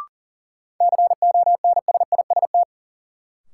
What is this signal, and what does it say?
Morse code, heard as a keyed tone, about 40 wpm, 700 Hz. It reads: CONSIST
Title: Sentences from Top 1400 Words - New Word Drill Alt - 40wpm